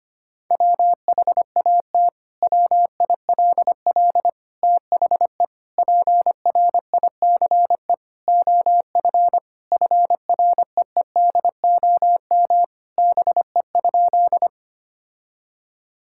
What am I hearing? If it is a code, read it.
Morse code, heard as a keyed tone, about 25 wpm, 700 Hz. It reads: WHAT WILL THE PRICE OF FREEDOM BE?
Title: Sentences from Top 1000 Words - Review Part 5 - 25wpm